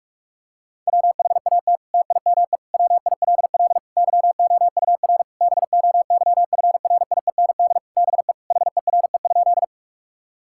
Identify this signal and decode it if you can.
Morse code, heard as a keyed tone, about 46 wpm, 700 Hz. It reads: WHAT TIME WILL YOUR BOYFRIEND BE HERE?
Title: Sentences from Top 1000 Words - Review Part 5 - 45wpm